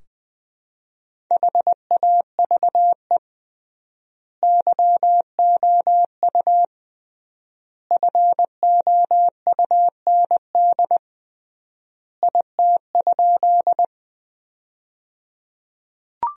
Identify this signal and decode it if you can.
Morse code, heard as a keyed tone, about 20 wpm, 700 Hz. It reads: HAVE YOU FOUND IT?
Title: 3x Word-Spacing - Sentences from Top 300 Words - 20wpm